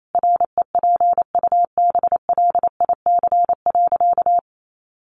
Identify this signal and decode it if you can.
Morse code, heard as a keyed tone, about 28 wpm, 700 Hz. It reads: REPUBLIC.
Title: Sentences from Top 1500 Words 28wpm